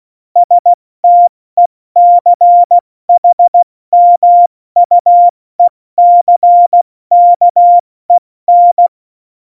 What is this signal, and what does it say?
Morse code, heard as a keyed tone, about 16 wpm, 700 Hz. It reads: STECHMUECKEN
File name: Morsekurs / Witze / Witz 25. September 2018, 70 F80